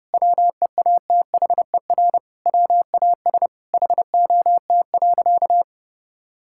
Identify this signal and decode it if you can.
Morse code, heard as a keyed tone, about 32 wpm, 700 Hz. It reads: WEATHER WAS HOT.
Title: Sentences from Top 1000 Words - Review Part 5 - 30wpm